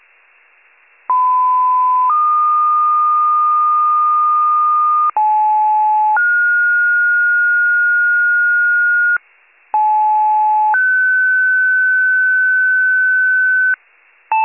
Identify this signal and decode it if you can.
Two-tone page, heard as a one-second tone, then a three-second tone, about 1010 then 1220 Hz, about 860 then 1430 Hz, about 860 then 1500 Hz, with noise. It is heard from 1.1 to 13.7 s.